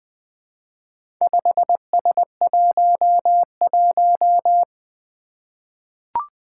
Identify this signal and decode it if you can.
Morse code, heard as a keyed tone, about 20 wpm, 700 Hz. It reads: 5S11
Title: Four Challenging Letter-Number Combinations 20wpm